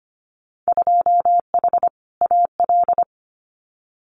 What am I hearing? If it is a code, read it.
Morse code, heard as a keyed tone, about 25 wpm, 690 Hz. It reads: 2H AL
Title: Winter Field Day Exchanges 25wpm